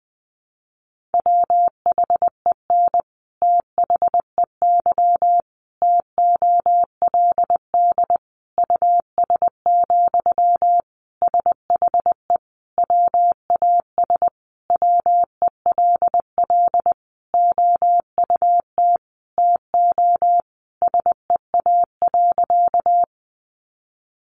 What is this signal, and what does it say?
Morse code, heard as a keyed tone, about 20 wpm, 700 Hz. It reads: WHEN THEY TOLD US, SHE WAS WELL OUT TO SEA.
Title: Sentences from Top 400 Words 20wpm